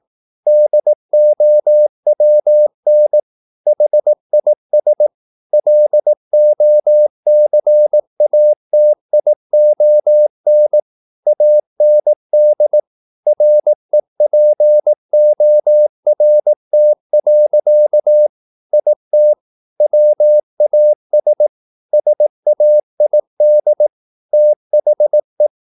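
Morse code, heard as a keyed tone, about 18 wpm, 600 Hz. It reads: DOWN HIS LOCATION AND REPORT. IT WAS SAID THE
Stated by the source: Morse practice at 18 wpm, English text, from CW med Gnister.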